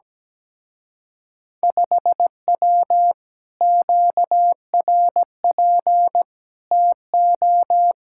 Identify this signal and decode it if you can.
Morse code, heard as a keyed tone, about 17 wpm, 700 Hz. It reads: 5W QRP TO
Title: Speed-Racing Common QSO Phrases - 17wpm